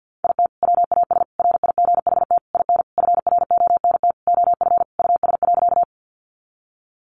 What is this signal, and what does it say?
Morse code, heard as a keyed tone, about 50 wpm, 700 Hz. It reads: IT WAS RIGHT IN FRONT OF US.
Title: Sentences from Top 500 Words 50wpm